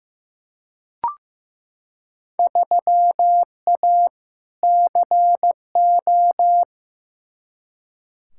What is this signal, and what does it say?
Morse code, heard as a keyed tone, about 15 wpm, 700 Hz. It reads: 3A CO
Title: Field Day Exchanges Encore - 15wpm